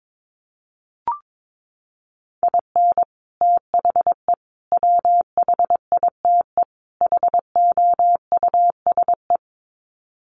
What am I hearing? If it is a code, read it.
Morse code, heard as a keyed tone, about 22 wpm, 700 Hz. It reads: IN THE WHITE HOUSE
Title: Sets of 4 Words 22wpm